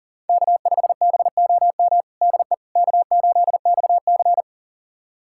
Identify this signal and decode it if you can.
Morse code, heard as a keyed tone, about 40 wpm, 700 Hz. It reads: K5BOM DE K8XC
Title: USA Callsign DE Callsign 40wpm